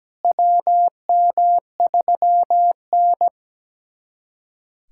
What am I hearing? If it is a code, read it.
Morse code, heard as a keyed tone, about 17 wpm, 700 Hz. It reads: WM3N
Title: USA Callsign DE Callsign - 4 Characters - 17wpm